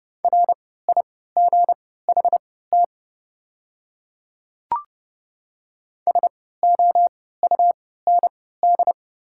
Morse code, heard as a keyed tone, about 32 wpm, spaced as for 17 wpm, 700 Hz. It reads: RIGHT SOUND
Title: 3x Character Spacing - Top 500 Words - 30wpm